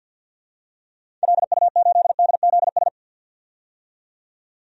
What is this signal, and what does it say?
Morse code, heard as a keyed tone, about 52 wpm, 700 Hz. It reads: LU8DZS
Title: DXCC-100 Call Signs - 6 Characters - 50wpm